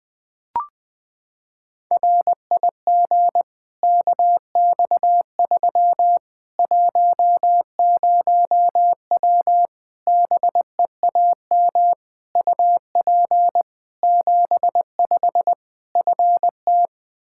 Morse code, heard as a keyed tone, about 20 wpm, 700 Hz. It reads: RIG KX3 10W BEAM UP 75 FT
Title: Practice QSO - Exchange 2 - Info - 20wpm